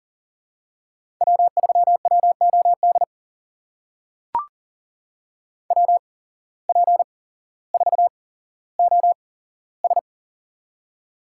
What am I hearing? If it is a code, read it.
Morse code, heard as a keyed tone, about 40 wpm, spaced as for 17 wpm, 700 Hz. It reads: W3WOD WP4OS
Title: USA Call Signs - 5 Characters - 8x Character Spacing - 40wpm